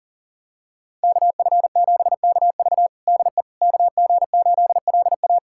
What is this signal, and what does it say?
Morse code, heard as a keyed tone, about 40 wpm, 700 Hz. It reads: KF7KV DE KG8LA
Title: USA Callsign DE Callsign - 5 Characters - 40wpm